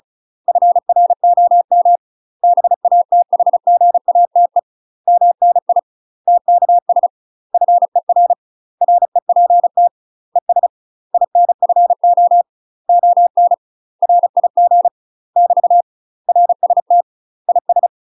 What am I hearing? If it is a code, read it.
Morse code, heard as a keyed tone, about 34 wpm, 700 Hz. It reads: FROM BATHGATE MNI TKS FER REPT ES INFO ON RIG <BT> RST IS
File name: UK MASTER QSOs TRAINING / 35WPM / UK MASTER QSOs 0024